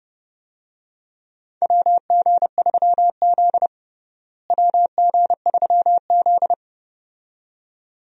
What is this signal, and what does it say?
Morse code, heard as a keyed tone, about 30 wpm, 700 Hz. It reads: WG3Z WG3Z
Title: USA Call Signs - 4 Characters - Repeated 2x - 30wpm